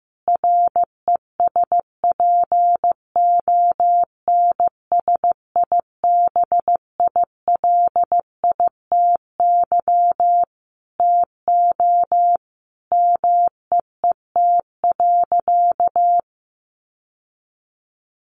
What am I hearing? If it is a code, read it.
Morse code, heard as a keyed tone, about 15 wpm, 700 Hz. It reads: RESPONSIBILITY TO MEET.
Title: Sentences from Top 1200 Words 15wpm